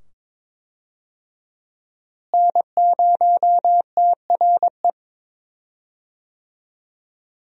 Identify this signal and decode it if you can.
Morse code, heard as a keyed tone, about 22 wpm, 700 Hz. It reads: N0TRE
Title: USA Call Signs - 5 Characters - Repeated 2x - 22wpm